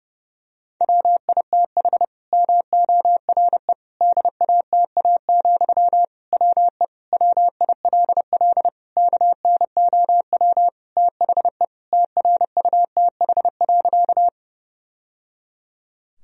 Morse code, heard as a keyed tone, about 30 wpm, 700 Hz. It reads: WITH MORE DATA, WE WILL KNOW THE TRUTH.
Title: Sentences from Top 1000 Words - Review Part 3 - 30wpm